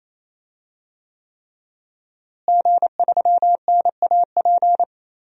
Morse code, heard as a keyed tone, about 28 wpm, 700 Hz. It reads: G3NAP
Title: DXCC-100 Call Signs - 5 Characters - Repeated 3x - 28wpm